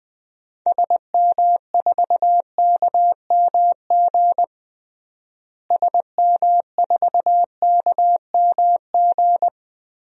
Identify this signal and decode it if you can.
Morse code, heard as a keyed tone, about 21 wpm, 700 Hz. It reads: SM4KMG SM4KMG
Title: DXCC-100 Call Signs - 6 Characters - Repeated 3x - 20wpm